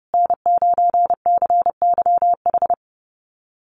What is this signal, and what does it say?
Morse code, heard as a keyed tone, about 30 wpm, 700 Hz. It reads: N9CYH
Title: USA Call Signs Repeated 3x 30wpm